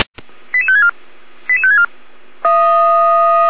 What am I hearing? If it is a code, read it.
Five-tone selective call (ZVEI), heard as quick runs of five tones, tones of about 80 ms. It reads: 88564 88564
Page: feuerwehr-orlamuende